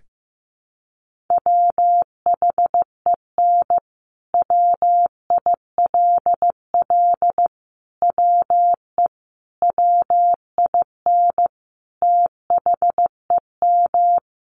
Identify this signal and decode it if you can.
Morse code, heard as a keyed tone, about 15 wpm, 700 Hz. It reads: WHEN WILL WE WIN THEM
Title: Sentences from Top 1500 Words 15wpm